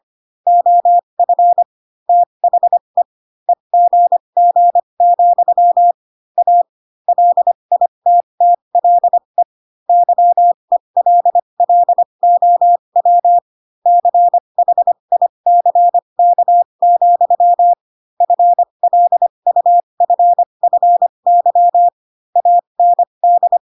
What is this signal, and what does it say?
Morse code, heard as a keyed tone, about 25 wpm, spaced as for 22 wpm, 700 Hz. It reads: OF THE EGG, A LITTLE YELLOW CHICK, FLUFFY AND